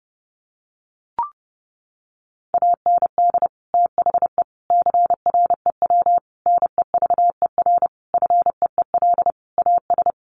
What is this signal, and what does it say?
Morse code, heard as a keyed tone, about 30 wpm, 700 Hz. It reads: AND THE CREW NEVER FEEL AS
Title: Sentences from Top 1500 Words 30wpm